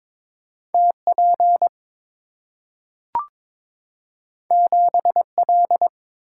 Morse code, heard as a keyed tone, about 22 wpm, 690 Hz. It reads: TP 7L
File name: practice / 22wpm / Two Letter-Number Combinations - 22wpm